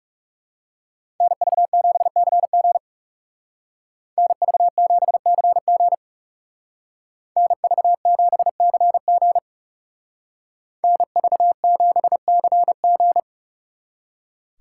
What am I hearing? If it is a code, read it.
Morse code, heard as a keyed tone, about 37 wpm, 700 Hz. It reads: NV7CG NV7CG NV7CG NV7CG
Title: Speed Racing - USA 2x2 Call Signs 30wpm